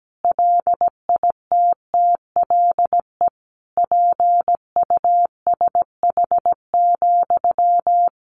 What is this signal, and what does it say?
Morse code, heard as a keyed tone, about 17 wpm, 700 Hz. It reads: LITTLE PUSH,
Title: Sentences from Top 1500 Words 17wpm